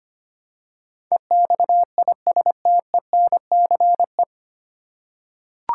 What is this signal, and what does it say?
Morse code, heard as a keyed tone, about 25 wpm, 700 Hz. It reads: EXISTENCE
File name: Nine Letter Words - 25wpm